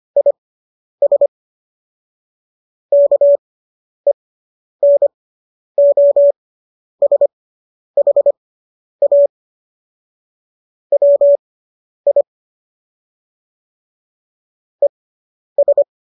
Morse code, heard as a keyed tone, about 25 wpm, spaced as for 10 wpm, 570 Hz. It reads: IS KENOSHA WI ES